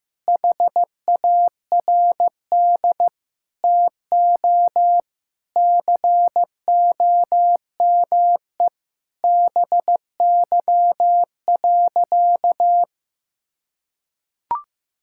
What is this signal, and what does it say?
Morse code, heard as a keyed tone, about 15 wpm, 700 Hz. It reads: HARD TO COME BY.
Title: Sentences from Top 1000 Words - Review Part 5 - 15wpm